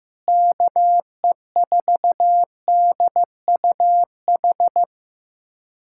Morse code, heard as a keyed tone, about 15 wpm, 700 Hz. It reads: KE4DUH
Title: USA Callsign DE Callsign - 6 Characters - 15wpm